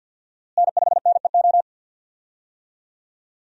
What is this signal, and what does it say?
Morse code, heard as a keyed tone, about 51 wpm, 700 Hz. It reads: N5NEO
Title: USA Call Signs - 5 Characters - Repeated 2x - 50wpm